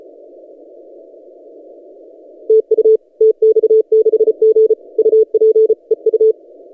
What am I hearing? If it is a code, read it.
Morse code, heard as a keyed tone, about 33 wpm, 440 Hz. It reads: TU TX6G UP EU
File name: tx6g-austral-is-12cw